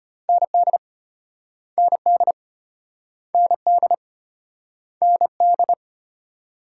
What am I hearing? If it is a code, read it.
Morse code, heard as a keyed tone, about 32 wpm, 700 Hz. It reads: ND ND ND ND